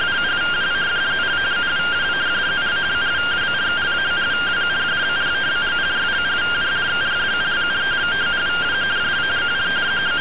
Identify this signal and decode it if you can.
Two-tone teleprinter signal, about 170 Hz shift.